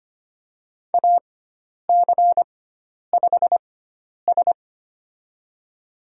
Morse code, heard as a keyed tone, about 25 wpm, spaced as for 10 wpm, 700 Hz. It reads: AC5S
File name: Lesson 16g - Letter c - Call Signs 25wpm